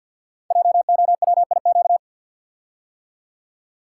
Morse code, heard as a keyed tone, about 50 wpm, 700 Hz. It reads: JOWIQ
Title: Code Groups - 5 Letters - 50wpm